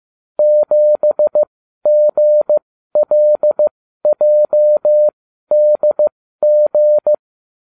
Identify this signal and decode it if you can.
Morse code, heard as a keyed tone, about 15 wpm, spaced as for 12 wpm, 600 Hz. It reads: 7GLJDG